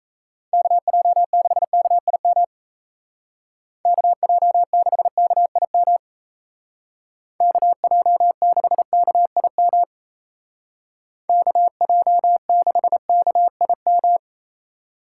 Morse code, heard as a keyed tone, about 33 wpm, 700 Hz. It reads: KJ6KIM KJ6KIM KJ6KIM KJ6KIM
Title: Speed Racing - USA 2x3 Call Signs 28wpm